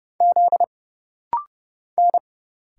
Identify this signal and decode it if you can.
Morse code, heard as a keyed tone, about 30 wpm, 700 Hz. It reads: Z N
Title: Single Letter-Number Alt - Rapid-Fire 30wpm